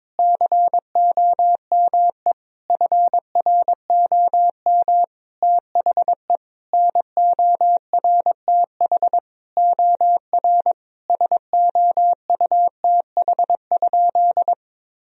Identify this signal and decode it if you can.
Morse code, heard as a keyed tone, about 22 wpm, 700 Hz. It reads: COME FROM THE NORTH OR SOUTH?
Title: Sentences from Top 1000 Words - Review Part 2 - 22wpm